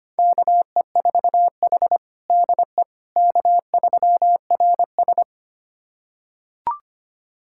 Morse code, heard as a keyed tone, about 25 wpm, 700 Hz. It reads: KE4H DE K3RS
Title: USA Callsign DE Callsign - 4 Characters - 25wpm